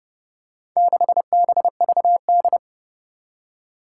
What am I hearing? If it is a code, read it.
Morse code, heard as a keyed tone, about 29 wpm, 700 Hz. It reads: 6BVD
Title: Four Challenging Letter-Number Combinations 30wpm